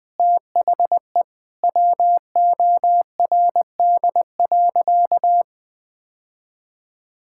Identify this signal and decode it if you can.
Morse code, heard as a keyed tone, about 20 wpm, 700 Hz. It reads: THE WORD.
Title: Sentences with 6-Word Phrases - Spoken Beginning - 20wpm